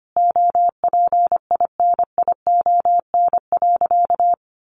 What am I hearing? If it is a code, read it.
Morse code, heard as a keyed tone, about 25 wpm, 700 Hz. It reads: OPINION.
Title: Sentences from Top 1200 Words 25wpm